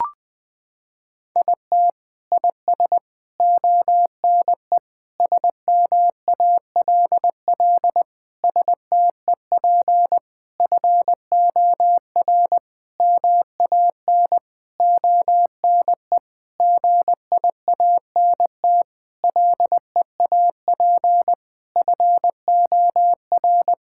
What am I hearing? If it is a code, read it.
Morse code, heard as a keyed tone, about 20 wpm, 700 Hz. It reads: IT IS ONE SMALL STEP FOR MAN ONE GIANT LEAP FOR
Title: Sentences with 9-Word Phrases - 20wpm